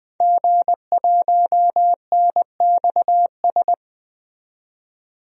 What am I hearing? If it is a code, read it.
Morse code, heard as a keyed tone, about 20 wpm, 700 Hz. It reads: G1NXS
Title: DXCC-100 Call Signs - 5 Characters - 20wpm